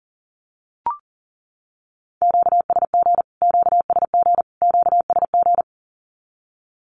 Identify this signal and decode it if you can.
Morse code, heard as a keyed tone, about 40 wpm, 700 Hz. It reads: QSG QSG QSG
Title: Q-Codes Repeated 3x 40wpm